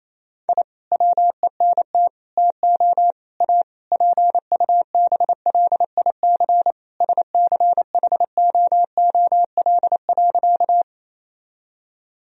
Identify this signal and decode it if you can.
Morse code, heard as a keyed tone, about 27 wpm, 700 Hz. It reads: I WENT TO A PUBLIC SCHOOL.
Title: Sentences from Top 1000 Words - Review Part 3 - 28wpm